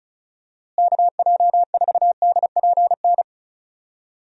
Morse code, heard as a keyed tone, about 34 wpm, 700 Hz. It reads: KJ4DPN
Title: USA Call Signs - 6 Characters - Repeated 3x - 35wpm